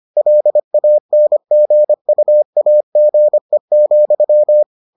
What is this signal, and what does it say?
Morse code, heard as a keyed tone, about 25 wpm, 600 Hz. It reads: LANGUAGE,